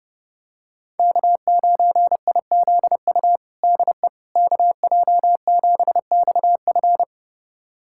Morse code, heard as a keyed tone, about 30 wpm, 700 Hz. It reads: K9IZU DE KJ7XF
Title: USA Callsign DE Callsign - 5 Characters - 30wpm